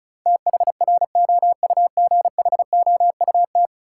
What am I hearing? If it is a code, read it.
Morse code, heard as a keyed tone, about 34 wpm, 700 Hz. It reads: THROUGHOUT
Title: Sentences from Top 800 Words - New Word Drill - 35wpm